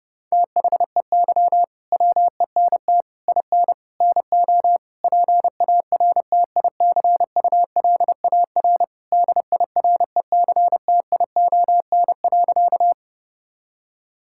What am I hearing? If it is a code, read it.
Morse code, heard as a keyed tone, about 30 wpm, 700 Hz. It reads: THEY WENT IN NO PARTICULAR DIRECTION.
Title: Sentences from Top 1000 Words - Review Part 5 - 30wpm